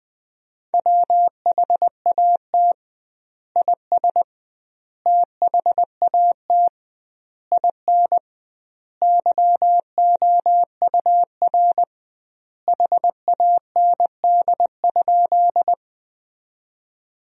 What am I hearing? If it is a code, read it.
Morse code, heard as a keyed tone, about 20 wpm, 700 Hz. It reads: WHAT IS THAT IN YOUR HAND?
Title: 2x Word-Spacing - Sentences from Top 200 Words - 20wpm